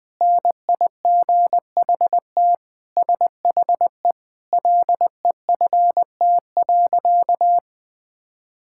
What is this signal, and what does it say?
Morse code, heard as a keyed tone, about 20 wpm, 700 Hz. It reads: NIGHT SHE LEFT.
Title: Sentences with 4-Word Phrases - 20wpm